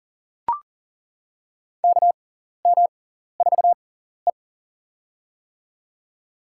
Morse code, heard as a keyed tone, about 38 wpm, spaced as for 14 wpm, 690 Hz. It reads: KM4E
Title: USA Call Signs - 4 Characters - 6x Character Spacing - 40wpm